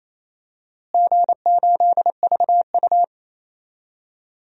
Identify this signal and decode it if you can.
Morse code, heard as a keyed tone, about 28 wpm, 700 Hz. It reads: G8VU
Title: DXCC-100 Call Signs - 4 Characters - Repeated 2x - 28wpm